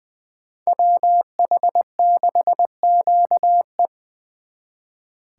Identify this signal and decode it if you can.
Morse code, heard as a keyed tone, about 24 wpm, 700 Hz. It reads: WH6QE
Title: Speed Racing - USA 2x2 Call Signs 20wpm